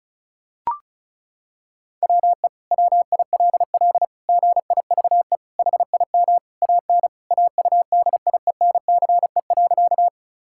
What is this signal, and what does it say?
Morse code, heard as a keyed tone, about 36 wpm, 700 Hz. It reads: WE WILL GIVE HIM AN AUDIENCE.
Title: Sentences from Top 1000 Words - Review Part 5 - 35wpm